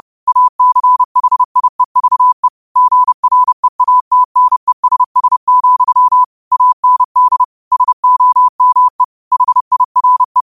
Morse code, heard as a keyed tone, about 30 wpm, 1000 Hz. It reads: ACHIEVE GREATNESS, AND SOME HIRE